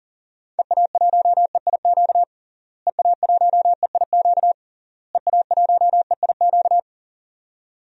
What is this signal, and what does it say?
Morse code, heard as a keyed tone, about 37 wpm, 700 Hz. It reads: EA1EIQ EA1EIQ EA1EIQ
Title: DXCC-100 Call Signs - 6 Characters - Repeated 3x - 40wpm